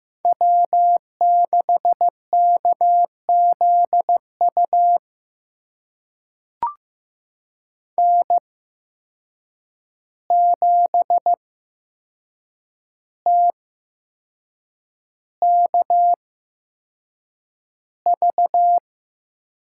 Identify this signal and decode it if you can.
Morse code, heard as a keyed tone, about 15 wpm, spaced as for 6 wpm, 700 Hz. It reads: W6KZU N7TKV
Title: USA Call Signs - 5 Characters - 8x Character Spacing - 15wpm